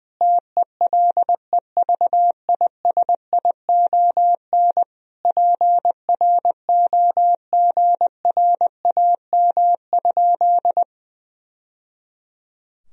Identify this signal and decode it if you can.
Morse code, heard as a keyed tone, about 20 wpm, 700 Hz. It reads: TELEVISION PROGRAM?
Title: Sentences from Top 1000 Words - Review Part 4 - 20wpm